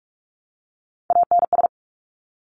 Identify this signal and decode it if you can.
Morse code, heard as a keyed tone, about 46 wpm, 700 Hz. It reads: ANS
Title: Top Three Letter Combinations in Words 45wpm